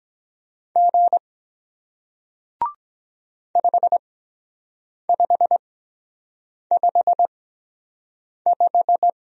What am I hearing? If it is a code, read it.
Morse code, heard as a keyed tone, about 23 wpm, spaced as for 7 wpm, 700 Hz. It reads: G 5555